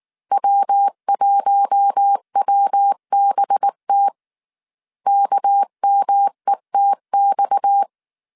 Morse code, heard as a keyed tone, about 19 wpm, 800 Hz. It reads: W1WBT KMETX
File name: CW / DET / 1 6 1500C S C 18wpm 2356